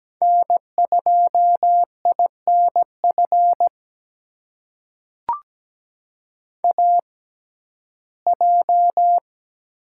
Morse code, heard as a keyed tone, about 17 wpm, spaced as for 12 wpm, 700 Hz. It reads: N2INF AJ
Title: USA Call Signs - 5 Characters - 6x Character Spacing - 17wpm